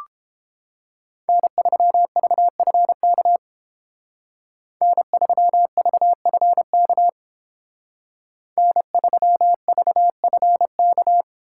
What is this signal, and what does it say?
Morse code, heard as a keyed tone, about 28 wpm, 700 Hz. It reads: N3VFK N3VFK N3VFK